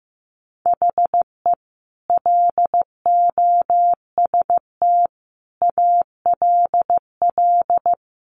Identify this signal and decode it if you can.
Morse code, heard as a keyed tone, about 15 wpm, 700 Hz. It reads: HE LOST ALL
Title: Sentences from Top 1400 Words 15wpm